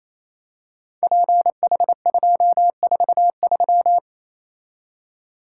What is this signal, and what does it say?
Morse code, heard as a keyed tone, about 27 wpm, 700 Hz. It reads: PH243
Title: Code Groups - 5 Letter:Numbers - 28wpm